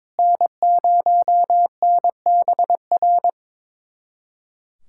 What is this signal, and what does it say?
Morse code, heard as a keyed tone, about 21 wpm, 700 Hz. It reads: N0NBR
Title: USA Callsign DE Callsign - 5 Characters - 22wpm